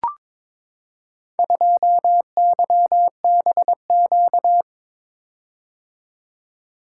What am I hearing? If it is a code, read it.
Morse code, heard as a keyed tone, about 22 wpm, 700 Hz. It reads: 2YBQ
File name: practice / 22wpm / Four Letter-Number Combinations - 22wpm